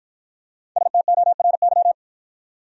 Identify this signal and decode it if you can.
Morse code, heard as a keyed tone, about 50 wpm, 700 Hz. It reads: STORY